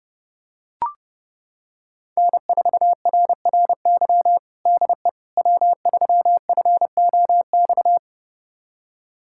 Morse code, heard as a keyed tone, about 32 wpm, 700 Hz. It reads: N4RRY DE W3FOX
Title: USA Callsign DE Callsign - 5 Characters - 30wpm